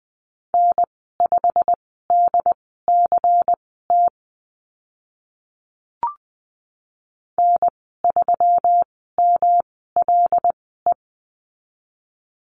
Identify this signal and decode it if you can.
Morse code, heard as a keyed tone, about 20 wpm, spaced as for 14 wpm, 700 Hz. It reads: N5DCT N3MLE
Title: 161 Lesson 26g - Letter g - Call Signs 20wpm